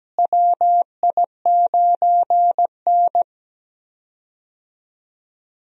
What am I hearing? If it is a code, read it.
Morse code, heard as a keyed tone, about 17 wpm, 700 Hz. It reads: WI9N
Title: USA Call Signs - 4 Characters - Repeated 3x - 17wpm